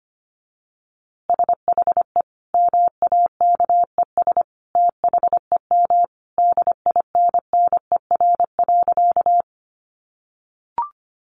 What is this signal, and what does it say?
Morse code, heard as a keyed tone, about 25 wpm, 700 Hz. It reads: SHE MAKES THEM DINNER.
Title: Easy Phrases 25wpm